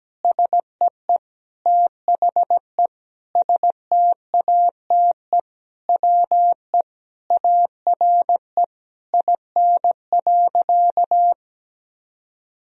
Morse code, heard as a keyed tone, about 17 wpm, 700 Hz. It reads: SEE THE STATE WE ARE IN.
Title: Sentences from Top 1000 Words - Review Part 2 - 17wpm